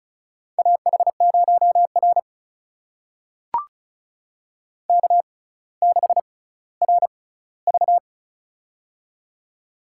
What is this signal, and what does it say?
Morse code, heard as a keyed tone, about 36 wpm, spaced as for 18 wpm, 700 Hz. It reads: AH0R K6RV
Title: USA Call Signs - 4 Characters - 6x Character Spacing - 35wpm